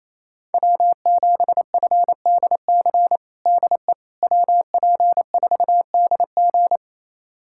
Morse code, heard as a keyed tone, about 28 wpm, 700 Hz. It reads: W7FDC DE WP4DG
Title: USA Callsign DE Callsign - 5 Characters - 28wpm